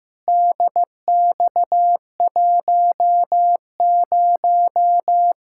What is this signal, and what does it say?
Morse code, heard as a keyed tone, about 15 wpm, 700 Hz. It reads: DX10
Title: Practice QSO - Exchange 2 - Info - 15wpm